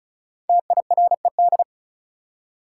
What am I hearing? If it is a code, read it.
Morse code, heard as a keyed tone, about 34 wpm, 700 Hz. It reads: TIRED
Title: Five Letter Words - 35wpm